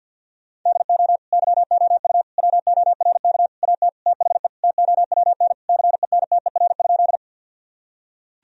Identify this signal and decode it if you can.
Morse code, heard as a keyed tone, about 50 wpm, 700 Hz. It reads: DO YOU WORK AT THE TOWN CENTER?
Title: Sentences from Top 1000 Words - Review Part 2 - 50wpm